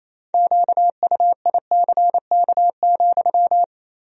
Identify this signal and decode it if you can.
Morse code, heard as a keyed tone, about 28 wpm, 700 Hz. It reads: QUICK,
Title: 3x Word-Spacing - Sentences from Top 500 Words - 28wpm